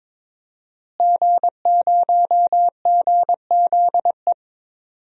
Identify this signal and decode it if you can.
Morse code, heard as a keyed tone, about 22 wpm, 700 Hz. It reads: G0GZE
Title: DXCC-100 Call Signs - 5 Characters - Repeated 3x - 22wpm